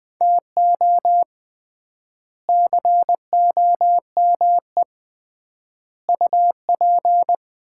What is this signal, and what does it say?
Morse code, heard as a keyed tone, about 20 wpm, 700 Hz. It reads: TO COME UP
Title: 3x Word-Spacing - Sets of 4 Words - 20wpm